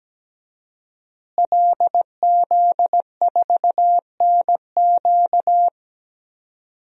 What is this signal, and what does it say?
Morse code, heard as a keyed tone, about 17 wpm, 700 Hz. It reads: LZ4NQ
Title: DXCC-100 Call Signs - 5 Characters - Repeated 3x - 17wpm